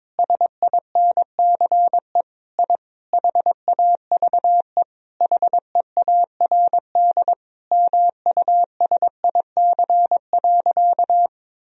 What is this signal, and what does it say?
Morse code, heard as a keyed tone, about 22 wpm, 700 Hz. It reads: SINCE I HAVE HEARD MUSIC.
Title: Sentences from Top 1000 Words - Review Part 1 - 22wpm